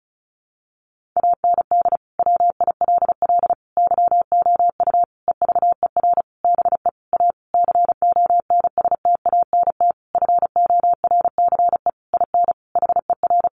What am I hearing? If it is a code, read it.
Morse code, heard as a keyed tone, about 34 wpm, 700 Hz. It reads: AND WILL YOU EVER BE A CONSTANT FORCE IN HER
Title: Sentences from Top 1500 Words 35wpm